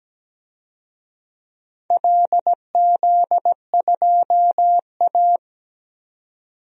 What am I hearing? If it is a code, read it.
Morse code, heard as a keyed tone, about 17 wpm, 700 Hz. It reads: LZ2A
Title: DXCC-100 Call Signs - 4 Characters - 17wpm